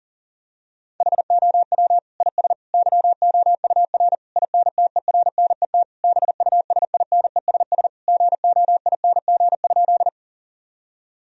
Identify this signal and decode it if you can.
Morse code, heard as a keyed tone, about 40 wpm, 700 Hz. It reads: HOW IS YOUR INTERNET BUSINESS GOING?
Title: Sentences from Top 1000 Words - Review Part 4 - 40wpm